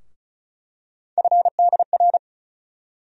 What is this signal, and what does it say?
Morse code, heard as a keyed tone, about 34 wpm, 700 Hz. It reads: FDR
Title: USA Call Sign Postfixes - 3 Letters 35wpm